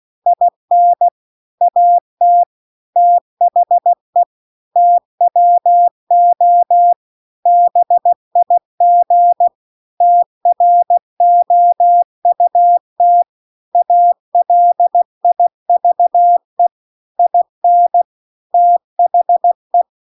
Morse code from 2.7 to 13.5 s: THE TWO BIG TROUT